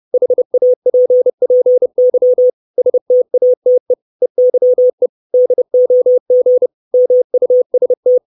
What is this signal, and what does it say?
Morse code, heard as a keyed tone, about 30 wpm, 500 Hz. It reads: HAPPY STATE EYE DOG MUST
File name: 632 Words 30WPM